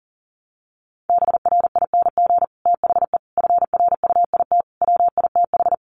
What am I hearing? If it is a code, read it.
Morse code, heard as a keyed tone, about 40 wpm, 700 Hz. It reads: BRING THE FRUIT WITH
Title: Sentences from Top 1300 Words 40wpm